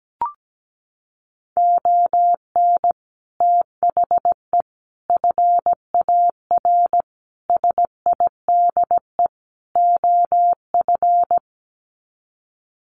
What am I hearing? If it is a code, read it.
Morse code, heard as a keyed tone, about 17 wpm, 700 Hz. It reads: ON THE FAR SIDE OF
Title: Sets of 5 Words 17wpm